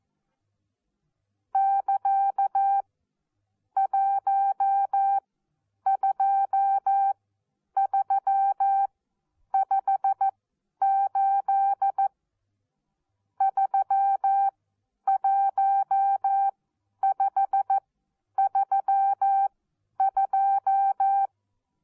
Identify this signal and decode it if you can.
Morse code, heard as a keyed tone, about 14 wpm, spaced as for 9 wpm, 780 Hz. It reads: <KA> 12358 31532
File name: Onlinelehrgang / morsen / DJ4UF Morsen14